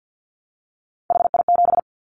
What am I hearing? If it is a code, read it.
Morse code, heard as a keyed tone, about 50 wpm, 700 Hz. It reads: HI7
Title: Callsign Prefixes for Top 100 DXCC - 50wpm